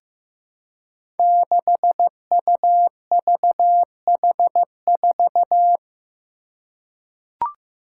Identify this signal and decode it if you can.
Morse code, heard as a keyed tone, about 15 wpm, 700 Hz. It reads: 6UVH4
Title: Five Challenging Letter-Number Combinations 15wpm